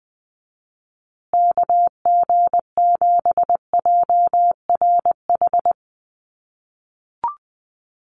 Morse code, heard as a keyed tone, about 19 wpm, 700 Hz. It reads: KG7JRH
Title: USA Call Signs Repeated 2x 20wpm